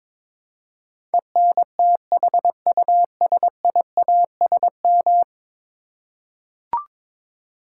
Morse code, heard as a keyed tone, about 22 wpm, 700 Hz. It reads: ENTHUSIASM
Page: morse-code-ninja-practice-22wpm